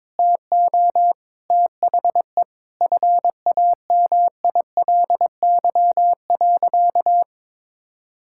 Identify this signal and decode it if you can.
Morse code, heard as a keyed tone, about 22 wpm, 700 Hz. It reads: TO THE FAMILY.
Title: Sentences from Top 1000 Words - Review Part 6 - 22wpm